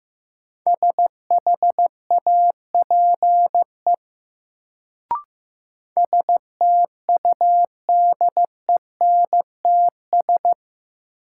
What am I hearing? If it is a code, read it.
Morse code, heard as a keyed tone, about 15 wpm, 700 Hz. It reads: SHAPE STUDENTS
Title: Sets of 5 Words Encore - New Word Drill - 15wpm